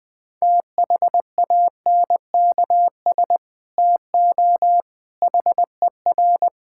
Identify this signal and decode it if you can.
Morse code, heard as a keyed tone, about 20 wpm, 700 Hz. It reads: THANKS TO HER
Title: Sentences from Top 1000 Words - Review Part 5 - 20wpm